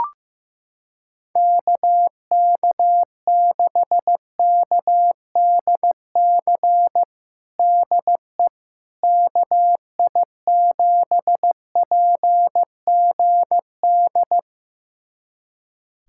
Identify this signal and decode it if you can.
Morse code, heard as a keyed tone, about 15 wpm, 700 Hz. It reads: KK6KDC DE KI7PGD
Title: USA Callsign DE Callsign 15wpm